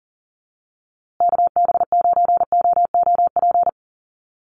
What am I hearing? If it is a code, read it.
Morse code, heard as a keyed tone, about 40 wpm, 700 Hz. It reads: KB9OOP
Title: USA Call Signs Repeated 3x 40wpm